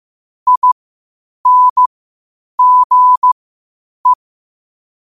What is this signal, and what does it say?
Morse code, heard as a keyed tone, about 15 wpm, spaced as for 8 wpm, 1000 Hz. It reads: INGE
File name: qotdfiles / 8wpm